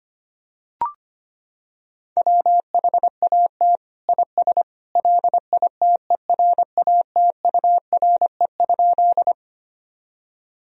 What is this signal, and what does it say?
Morse code, heard as a keyed tone, about 25 wpm, 700 Hz. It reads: WHAT IS LITERATURE?
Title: Sentences from Top 1000 Words - Review Part 3 - 25wpm